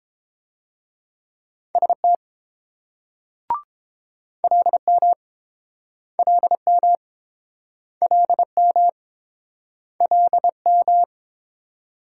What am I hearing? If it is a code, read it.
Morse code, heard as a keyed tone, about 29 wpm, 700 Hz. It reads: ST LM LM LM LM